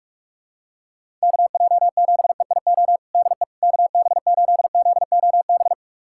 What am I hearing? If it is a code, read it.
Morse code, heard as a keyed tone, about 41 wpm, 700 Hz. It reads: KJ7EIO DE KB8ZOB
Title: USA Callsign DE Callsign 45wpm